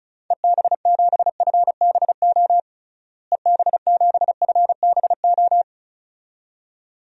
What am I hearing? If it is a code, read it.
Morse code, heard as a keyed tone, about 36 wpm, 700 Hz. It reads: EB7FBO EB7FBO
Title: DXCC-100 Call Signs - 6 Characters - Repeated 2x - 35wpm